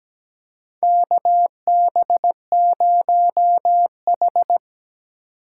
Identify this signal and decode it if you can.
Morse code, heard as a keyed tone, about 17 wpm, 700 Hz. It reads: KB0H
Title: USA Call Signs - 4 Characters - 2x Character Spacing - 17wpm